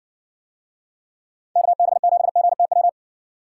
Morse code, heard as a keyed tone, about 54 wpm, 700 Hz. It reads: KB7GTW